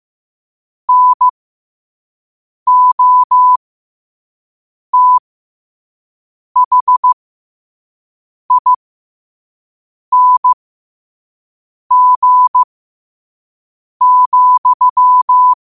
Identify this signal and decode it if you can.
Morse code, heard as a keyed tone, about 15 wpm, spaced as for 5 wpm, 1000 Hz. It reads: NOTHING,
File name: qotdfiles / 5wpm